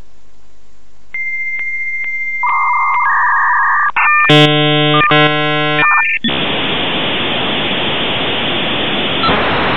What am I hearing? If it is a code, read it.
Dial-up modem handshake.